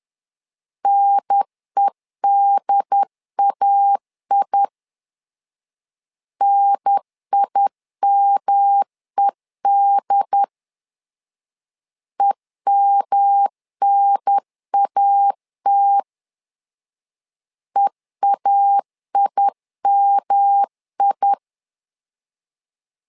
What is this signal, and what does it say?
Morse code, heard as a keyed tone, about 11 wpm, 800 Hz. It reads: NEDAI NIMED EMNAT EAIMI
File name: LEZIONI / 3 MWRO / 3 M 10wpm 1828